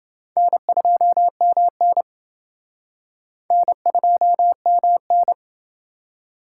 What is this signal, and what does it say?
Morse code, heard as a keyed tone, about 29 wpm, 700 Hz. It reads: N2MN N2MN